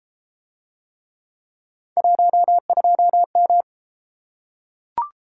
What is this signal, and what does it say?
Morse code, heard as a keyed tone, about 32 wpm, 700 Hz. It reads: 12M